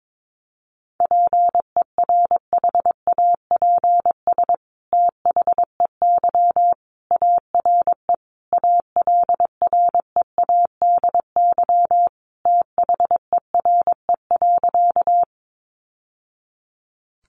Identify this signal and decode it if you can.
Morse code, heard as a keyed tone, about 22 wpm, 700 Hz. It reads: PERHAPS THEY ARE ALREADY THERE.
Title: Sentences from Top 700 Words 22wpm